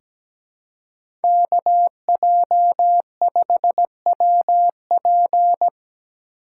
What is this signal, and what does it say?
Morse code, heard as a keyed tone, about 17 wpm, 700 Hz. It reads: KJ5WP
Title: USA Call Signs - 5 Characters - 3x Character Spacing - 17wpm